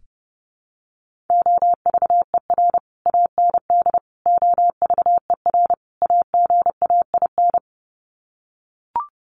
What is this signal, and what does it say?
Morse code, heard as a keyed tone, about 30 wpm, 700 Hz. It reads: OVER AND OVER AGAIN
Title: Sets of 4 Words 30wpm